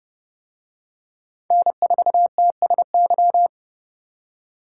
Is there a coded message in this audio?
Yes: Morse code from 1.5 to 3.5 s, heard as a keyed tone, about 29 wpm, 700 Hz. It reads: N4TSY